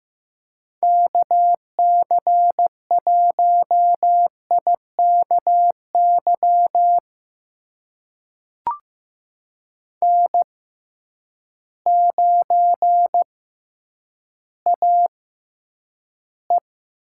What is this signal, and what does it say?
Morse code, heard as a keyed tone, about 15 wpm, spaced as for 9 wpm, 700 Hz. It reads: KC1IKY N9AE
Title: USA Call Signs - 6x Character Spacing - 15wpm